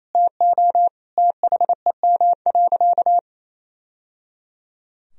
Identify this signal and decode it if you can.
Morse code, heard as a keyed tone, about 28 wpm, 700 Hz. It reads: TO THEM.
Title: Sentences from Top 1000 Words - Review Part 3 - 28wpm